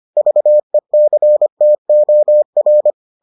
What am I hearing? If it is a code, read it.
Morse code, heard as a keyed tone, about 25 wpm, 600 Hz. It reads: VECTOR